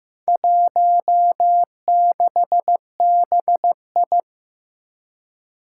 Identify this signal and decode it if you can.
Morse code, heard as a keyed tone, about 15 wpm, 700 Hz. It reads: 16BI
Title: Four Challenging Letter-Number Combinations 15wpm